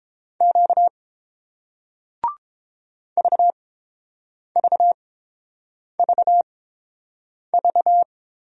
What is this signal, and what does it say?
Morse code, heard as a keyed tone, about 30 wpm, spaced as for 7 wpm, 700 Hz. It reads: Q VVVV